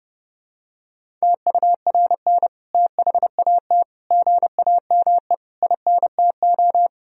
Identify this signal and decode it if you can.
Morse code, heard as a keyed tone, about 32 wpm, 700 Hz. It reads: TURN THAT GAME INTO
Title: Sentences from Top 1000 Words - Review Part 3 - 30wpm